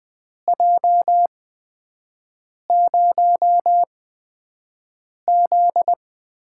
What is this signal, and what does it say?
Morse code, heard as a keyed tone, about 20 wpm, spaced as for 5 wpm, 700 Hz. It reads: J0Z
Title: USA Call Signs - 8x Character Spacing - 20wpm